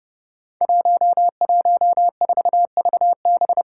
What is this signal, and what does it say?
Morse code, heard as a keyed tone, about 30 wpm, 700 Hz. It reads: 114VB
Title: Five Challenging Letter-Number Combinations 30wpm